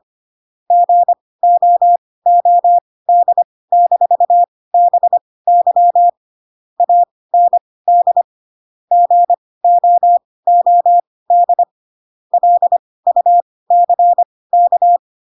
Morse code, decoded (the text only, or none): GOOD-BY AND GOOD LUCK